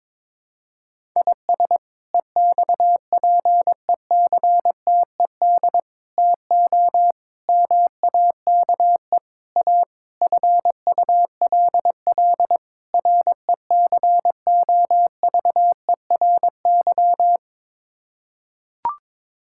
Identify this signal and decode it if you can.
Morse code, heard as a keyed tone, about 22 wpm, 700 Hz. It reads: IS EXPECTED TO MAKE A FULL RECOVERY
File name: practice / 22wpm / Sets of 7 Words - 22wpm